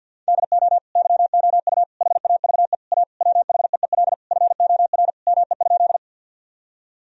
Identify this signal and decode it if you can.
Morse code, heard as a keyed tone, about 53 wpm, 700 Hz. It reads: DO YOU HAVE A WHEEL FOR ME?
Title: Sentences from Top 1000 Words - Review Part 3 - 50wpm